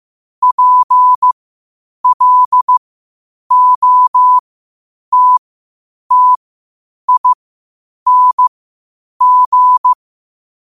Morse code, heard as a keyed tone, about 15 wpm, spaced as for 8 wpm, 1000 Hz. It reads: PLOTTING